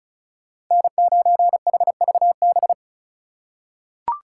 Morse code, heard as a keyed tone, about 34 wpm, 700 Hz. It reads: N9HVB
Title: USA Call Signs - 5 Characters - 2x Character Spacing - 35wpm